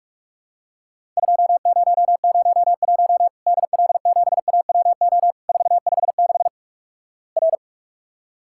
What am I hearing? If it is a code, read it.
Morse code, heard as a keyed tone, about 46 wpm, 700 Hz. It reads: 2001 DL7AWO 456 R
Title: Worked All Europe DX Contest - QTC Exchange - 45wpm